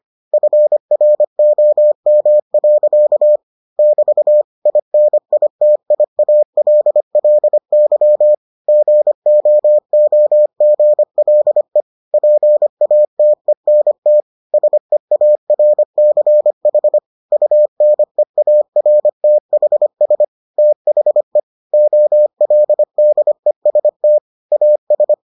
Morse code, heard as a keyed tone, about 25 wpm, 600 Hz. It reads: FROM. <BT> INITIALLY GOOGLE PATENT SEARCH UNEARTHS THE OLDEST AS